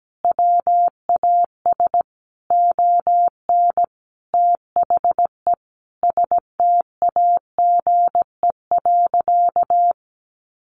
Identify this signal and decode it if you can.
Morse code, heard as a keyed tone, about 17 wpm, 700 Hz. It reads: WAS ON THE STAGE.
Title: Sentences from Top 1200 Words 17wpm